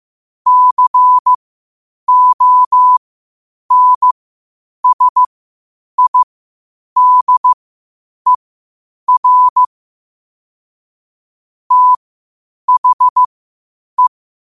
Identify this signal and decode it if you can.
Morse code, heard as a keyed tone, about 15 wpm, spaced as for 8 wpm, 1000 Hz. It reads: CONSIDER THE
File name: qotdfiles / 8wpm